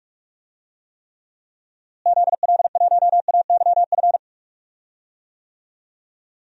Morse code, heard as a keyed tone, about 46 wpm, 700 Hz. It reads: ZL1AYF
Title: DXCC-100 Call Signs - 6 Characters - Repeated 3x - 45wpm